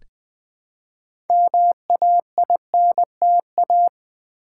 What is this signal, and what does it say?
Morse code, heard as a keyed tone, about 20 wpm, 700 Hz. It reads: MAINTA
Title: Build A Word - 2 Letters At A Time - 8-Letter Words - 20wpm